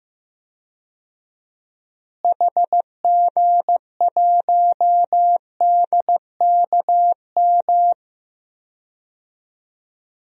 Morse code, heard as a keyed tone, about 15 wpm, 700 Hz. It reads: HG1DKM
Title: DXCC-100 Call Signs - 6 Characters - Repeated 3x - 15wpm